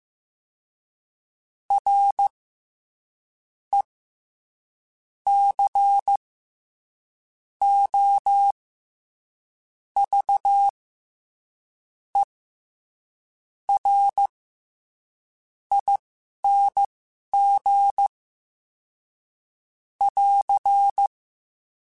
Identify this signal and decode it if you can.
Morse code, heard as a keyed tone, about 15 wpm, spaced as for 6 wpm, 800 Hz. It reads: RECOVERING <AR>
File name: Special INGs-mp3s / Special INGs-mp3s / SlowToFaster ing @10